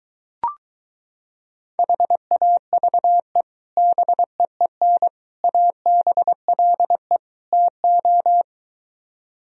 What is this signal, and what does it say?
Morse code, heard as a keyed tone, about 24 wpm, 700 Hz. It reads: HAVE BEEN ABLE TO